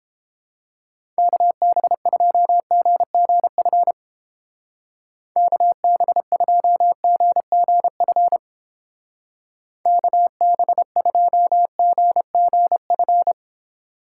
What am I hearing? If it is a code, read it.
Morse code, heard as a keyed tone, about 30 wpm, 700 Hz. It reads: KB2GGF KB2GGF KB2GGF